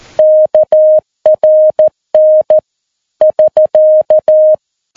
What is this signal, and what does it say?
Morse code, heard as a keyed tone, about 14 wpm, 620 Hz. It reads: KRN <SK>
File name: PI4AA Uitzending